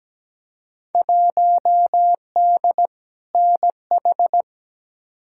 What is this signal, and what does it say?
Morse code, heard as a keyed tone, about 17 wpm, 700 Hz. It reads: 1D NH
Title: Field Day Exchanges - 17wpm